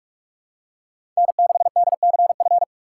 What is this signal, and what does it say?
Morse code, heard as a keyed tone, about 43 wpm, 700 Hz. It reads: N6DCF
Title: USA Call Signs - 5 Characters - 2x Character Spacing - 45wpm